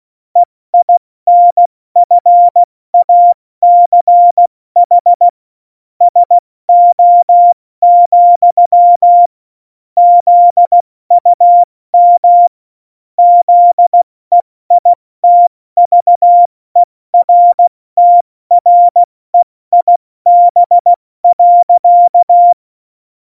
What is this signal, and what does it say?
Morse code, heard as a keyed tone, about 16 wpm, spaced as for 14 wpm, 700 Hz. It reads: EINFACH SO, ZUM ZEITVERTREIB.
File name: Morsekurs / Witze / Witz 19. Januar 2021, 70 F80